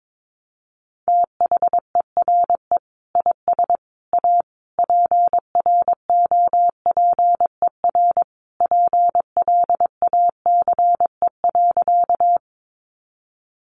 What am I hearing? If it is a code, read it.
Morse code, heard as a keyed tone, about 22 wpm, 700 Hz. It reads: THERE IS A PROPER PLACE.
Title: Sentences from Top 800 Words 22wpm